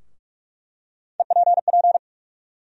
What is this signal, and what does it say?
Morse code, heard as a keyed tone, about 46 wpm, 700 Hz. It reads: EPP
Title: USA Call Sign Postfixes - 3 Letters 45wpm